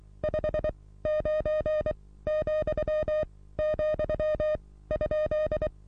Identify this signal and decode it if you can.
Morse code, heard as a keyed tone, about 23 wpm, spaced as for 16 wpm, 610 Hz. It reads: 59,,?
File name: repos / Morsekurs / DJ4UF Morsen27